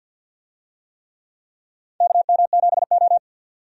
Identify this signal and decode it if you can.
Morse code, heard as a keyed tone, about 52 wpm, 700 Hz. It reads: KM7O
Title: USA Call Signs - 4 Characters - 50wpm